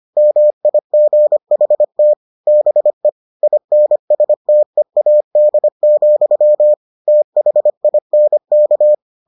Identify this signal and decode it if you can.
Morse code, heard as a keyed tone, about 25 wpm, 600 Hz. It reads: MIGHT BE INSTEAD, THINK